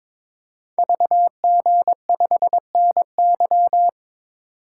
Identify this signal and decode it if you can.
Morse code, heard as a keyed tone, about 22 wpm, 700 Hz. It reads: VG5NY
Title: Code Groups - 5 Letter:Numbers - 22wpm